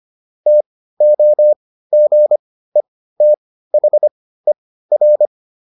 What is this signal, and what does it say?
Morse code, heard as a keyed tone, about 25 wpm, spaced as for 15 wpm, 600 Hz. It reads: TOGETHER
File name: SS215 15